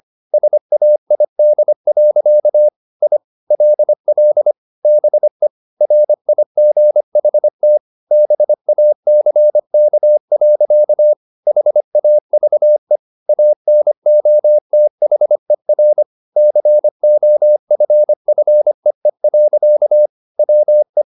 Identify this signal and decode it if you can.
Morse code, heard as a keyed tone, about 25 wpm, 600 Hz. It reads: SAID. I LL BE RIGHT BACK. HAVE ANOTHER COFFEE. WE
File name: the old man and the sea part 1 25wpm